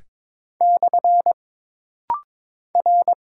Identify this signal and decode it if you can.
Morse code, heard as a keyed tone, about 22 wpm, 700 Hz. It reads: / R
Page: This rapid-fire timing is geared to push your ICR (Instant Character Recognition) skills.